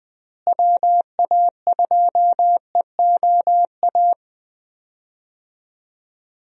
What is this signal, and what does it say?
Morse code, heard as a keyed tone, about 20 wpm, 700 Hz. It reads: WA2EOA
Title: USA Call Signs - 6 Characters - Repeated 3x - 20wpm